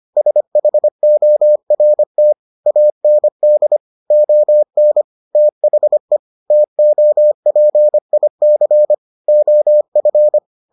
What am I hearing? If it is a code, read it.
Morse code, heard as a keyed tone, about 25 wpm, 600 Hz. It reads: SHORT AND ON THE TOPIC OF